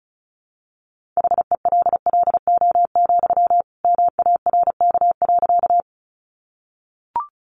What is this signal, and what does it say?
Morse code, heard as a keyed tone, about 34 wpm, 700 Hz. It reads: HELLO, MARK.
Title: Sentences from Top 1100 Words 35wpm